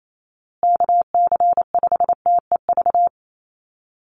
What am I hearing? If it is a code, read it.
Morse code, heard as a keyed tone, about 28 wpm, 700 Hz. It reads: KC5TEV
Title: USA Call Signs Repeated 2x 28wpm